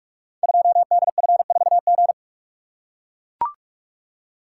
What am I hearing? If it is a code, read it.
Morse code, heard as a keyed tone, about 44 wpm, 700 Hz. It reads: 2DF4G
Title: Code Groups - 5 Letter:Numbers - 45wpm